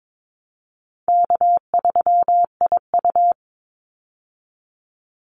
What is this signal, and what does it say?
Morse code, heard as a keyed tone, about 22 wpm, 700 Hz. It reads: K3IU
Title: Easy Call Signs Repeated 3x - 22wpm